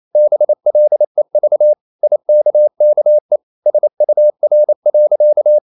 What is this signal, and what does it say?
Morse code, heard as a keyed tone, about 28 wpm, 600 Hz. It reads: BLEV IKKE SUR.
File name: Ferie QRQ 28wpm